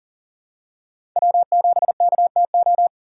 Morse code, heard as a keyed tone, about 40 wpm, 700 Hz. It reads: W7KTO
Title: USA Call Signs - 3x Character Spacing - 40wpm